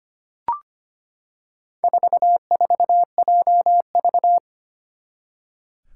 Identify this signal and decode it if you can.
Morse code, heard as a keyed tone, about 25 wpm, 700 Hz. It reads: 44JV
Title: Four Challenging Letter-Number Combinations 25wpm